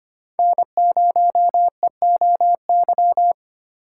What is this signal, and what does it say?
Morse code, heard as a keyed tone, about 25 wpm, 700 Hz. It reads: N0EOY
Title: USA Call Signs - 8x Character Spacing - 25wpm